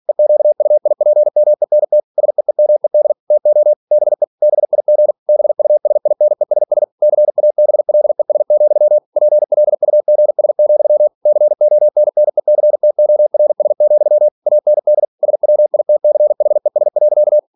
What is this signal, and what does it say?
Morse code, heard as a keyed tone, about 46 wpm, 600 Hz. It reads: EQUIPMENT SEEMED TO BE BIG BUSINESS CABLES, PLUGS, CONNECTORS, AND SWITCHES.
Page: Never 47wpm